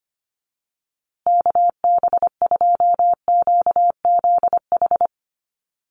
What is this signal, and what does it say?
Morse code, heard as a keyed tone, about 25 wpm, 700 Hz. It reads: KB2QZH
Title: USA Call Signs Repeated 2x 25wpm